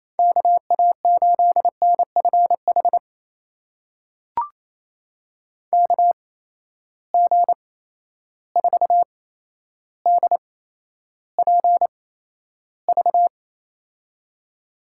Morse code, heard as a keyed tone, about 28 wpm, spaced as for 12 wpm, 700 Hz. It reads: KA8NFH KG4DPV
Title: USA Call Signs - 6 Characters - 8x Character Spacing - 28wpm